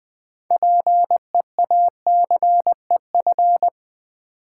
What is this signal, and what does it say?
Morse code, heard as a keyed tone, about 20 wpm, 700 Hz. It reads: PEACEF
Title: Build A Word - 2 Letters At A Time - 8-Letter Words - 20wpm